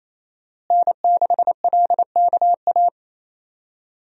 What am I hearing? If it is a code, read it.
Morse code, heard as a keyed tone, about 28 wpm, 700 Hz. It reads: N6LKA
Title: USA Call Signs - 5 Characters - Repeated 3x - 28wpm